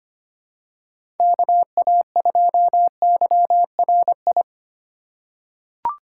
Morse code, heard as a keyed tone, about 25 wpm, 700 Hz. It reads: KA2YRI
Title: USA Call Signs - 6 Characters - 2x Character Spacing - 25wpm